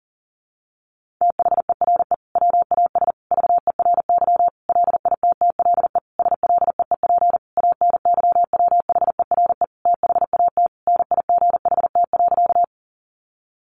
Morse code, heard as a keyed tone, about 40 wpm, 700 Hz. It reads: THERE WAS VERY LITTLE SLEEP ANYWHERE THAT NIGHT.
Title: Sentences from Top 1400 Words 40wpm